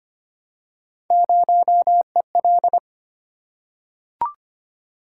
Morse code, heard as a keyed tone, about 25 wpm, 700 Hz. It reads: 0EL
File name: Three Letter-Number Combinations - 25wpm